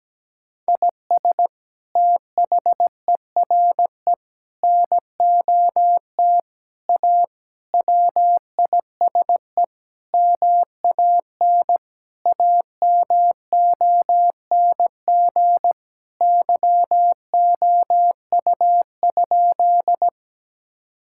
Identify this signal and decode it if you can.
Morse code, heard as a keyed tone, about 17 wpm, 700 Hz. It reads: IS THERE NOT A WISE MAN AMONG YOU?
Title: Sentences from Top 1600 Words 17wpm